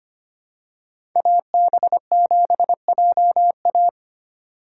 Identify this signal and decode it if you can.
Morse code, heard as a keyed tone, about 25 wpm, 700 Hz. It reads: AB7JA
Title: USA Call Signs - 5 Characters - 2x Character Spacing - 25wpm